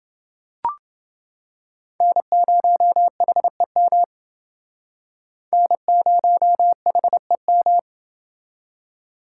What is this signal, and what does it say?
Morse code, heard as a keyed tone, about 28 wpm, 700 Hz. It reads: N0HEM N0HEM